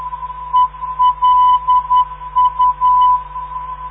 Morse code, heard as a keyed tone, about 10 wpm, 1000 Hz. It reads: ELU
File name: ndb countryes / zvoki mp3 / elu 368khz mp3